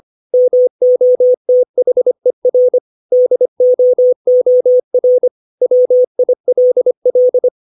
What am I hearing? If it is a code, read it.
Morse code, heard as a keyed tone, about 25 wpm, 500 Hz. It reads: MOTHER DOOR WILL
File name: 632 words / 632 Words 25WPM